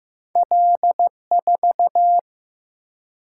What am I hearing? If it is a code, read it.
Morse code, heard as a keyed tone, about 15 wpm, 700 Hz. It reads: L4
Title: Single-Letter Single-Number 15wpm